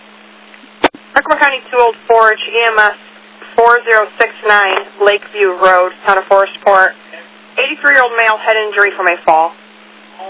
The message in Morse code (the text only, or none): none